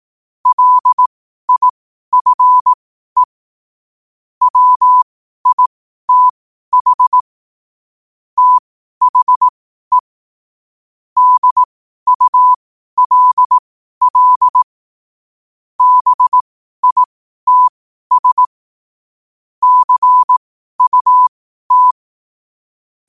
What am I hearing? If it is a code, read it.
Morse code, heard as a keyed tone, about 18 wpm, spaced as for 12 wpm, 1000 Hz. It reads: LIFE WITH THE DULL BITS CUT
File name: qotdfiles / 12wpm